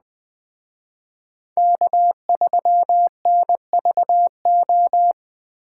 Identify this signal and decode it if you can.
Morse code, heard as a keyed tone, about 20 wpm, 700 Hz. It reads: K3NVO
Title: Speed Racing - USA 1x3 Call Signs 17wpm